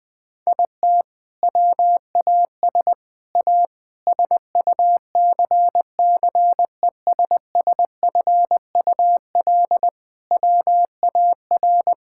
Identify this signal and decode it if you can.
Morse code, heard as a keyed tone, about 20 wpm, 700 Hz. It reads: IT WAS A SUCCESSFUL WAR
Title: Sentences from Top 1000 Words - Review Part 6 - 20wpm